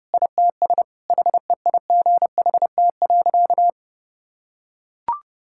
Morse code, heard as a keyed tone, about 32 wpm, 700 Hz. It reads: ITS HEIGHT.
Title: Sentences from Top 1600 Words 30wpm